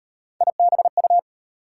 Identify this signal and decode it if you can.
Morse code, heard as a keyed tone, about 36 wpm, 700 Hz. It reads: IBU